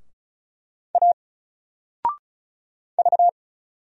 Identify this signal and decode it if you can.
Morse code, heard as a keyed tone, about 34 wpm, 700 Hz. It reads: A V
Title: Single Letter-Number Alt 35wpm